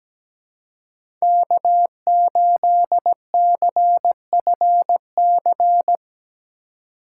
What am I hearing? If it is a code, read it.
Morse code, heard as a keyed tone, about 17 wpm, 700 Hz. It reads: K8CFC
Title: USA Call Signs - 5 Characters - 4x Character Spacing - 17wpm